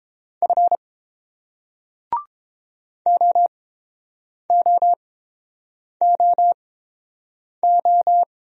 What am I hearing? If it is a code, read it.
Morse code, heard as a keyed tone, about 28 wpm, spaced as for 7 wpm, 700 Hz. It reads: F OOOO